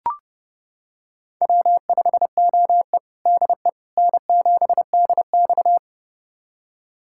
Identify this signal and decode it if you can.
Morse code, heard as a keyed tone, about 30 wpm, 700 Hz. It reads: W5OE DE N7DX
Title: USA Callsign DE Callsign - 4 Characters - 30wpm